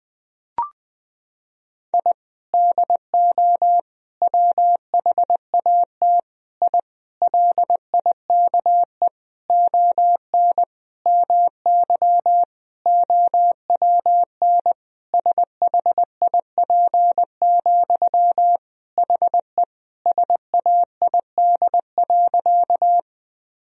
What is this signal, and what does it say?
Morse code, heard as a keyed tone, about 20 wpm, 700 Hz. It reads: I DO WHAT I LIKE ON MY OWN SHIP, HE SAID.
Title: Sentences from Top 1000 Words - Review Part 2 - 20wpm